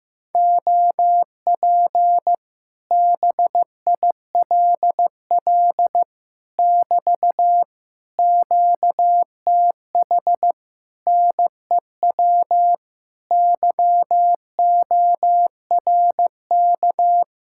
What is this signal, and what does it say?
Morse code, heard as a keyed tone, about 15 wpm, 700 Hz. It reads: OP BILL <BT> QTH NEW YORK
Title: Speed-Racing Common QSO Phrases Encore - 15wpm